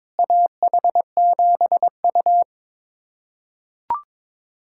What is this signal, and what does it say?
Morse code, heard as a keyed tone, about 22 wpm, 700 Hz. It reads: AH7U
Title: USA Callsign DE Callsign - 4 Characters - 22wpm